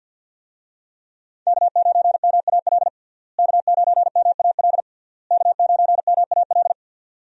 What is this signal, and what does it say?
Morse code, heard as a keyed tone, about 48 wpm, 700 Hz. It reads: K9MAL K9MAL K9MAL
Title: USA Call Signs - 5 Characters - Repeated 3x - 50wpm